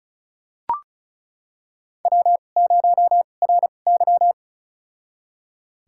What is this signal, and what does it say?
Morse code, heard as a keyed tone, about 36 wpm, spaced as for 25 wpm, 700 Hz. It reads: W0RY
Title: USA Call Signs - 4 Characters - 2x Character Spacing - 35wpm